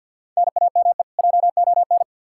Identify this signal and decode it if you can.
Morse code, heard as a keyed tone, about 49 wpm, 700 Hz. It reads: NAME JON
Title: K1USN SST QSO Practice 50wpm